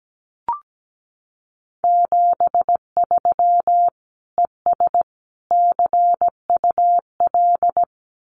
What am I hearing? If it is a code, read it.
Morse code, heard as a keyed tone, about 17 wpm, 700 Hz. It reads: 73 ES CUL
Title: Common QSO Phrases Encore 17wpm